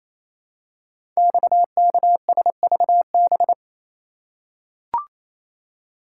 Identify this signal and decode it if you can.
Morse code, heard as a keyed tone, about 27 wpm, 700 Hz. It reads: XKSVB
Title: Code Groups - 5 Letters - 28wpm